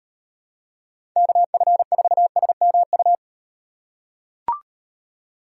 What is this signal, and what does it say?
Morse code, heard as a keyed tone, about 38 wpm, 700 Hz. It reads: KF4SMU